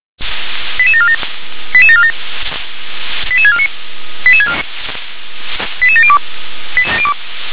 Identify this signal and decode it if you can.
Five-tone selective call (ZVEI), heard as quick runs of five tones, tones of about 70 ms. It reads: 88747 88747 88649 88702